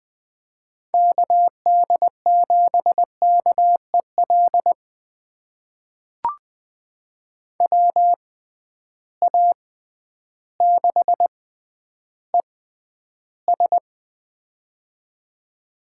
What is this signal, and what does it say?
Morse code, heard as a keyed tone, about 20 wpm, spaced as for 11 wpm, 700 Hz. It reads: KD7KEL WA6ES
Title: USA Call Signs - 6x Character Spacing - 20wpm